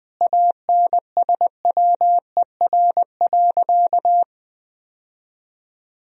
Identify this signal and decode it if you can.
Morse code, heard as a keyed tone, about 20 wpm, 700 Hz. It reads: ANSWER.
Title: Sentences from Top 1000 Words - Review Part 4 - 20wpm